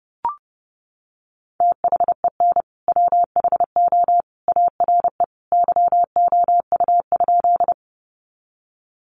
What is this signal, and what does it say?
Morse code, heard as a keyed tone, about 30 wpm, 700 Hz. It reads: THEN WHO ARE YOU?
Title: Sentences from Top 100 Words 30wpm